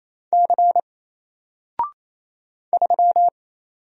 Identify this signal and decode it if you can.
Morse code, heard as a keyed tone, about 27 wpm, 700 Hz. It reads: C 3
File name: Single-Letter-Number-28wpm